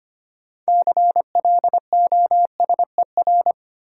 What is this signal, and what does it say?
Morse code, heard as a keyed tone, about 25 wpm, 700 Hz. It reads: CLOSER
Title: Sets of 5 Words Encore - New Word Drill - 25wpm